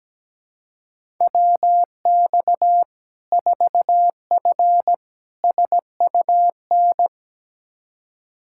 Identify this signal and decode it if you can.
Morse code, heard as a keyed tone, about 17 wpm, 700 Hz. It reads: WX 4F SUN
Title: Practice QSO - Exchange 2 - Info - 17wpm